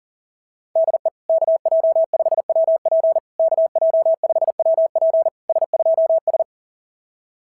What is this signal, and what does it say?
Morse code, heard as a keyed tone, about 40 wpm, 690 Hz. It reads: DE KJ5WP KJ5WP S2S
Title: SOTA QSO Practice 40wpm